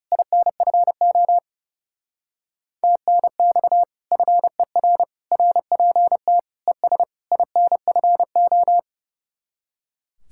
Morse code, heard as a keyed tone, about 36 wpm, 700 Hz. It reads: INFO TNX FER RPT ES INFO
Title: Speed-Racing Common QSO Phrases - 30wpm